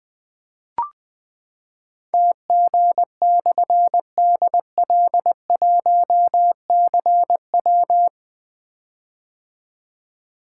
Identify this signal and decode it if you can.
Morse code, heard as a keyed tone, about 20 wpm, 700 Hz. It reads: TG/DL1CW
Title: Difficult Call Signs 20wpm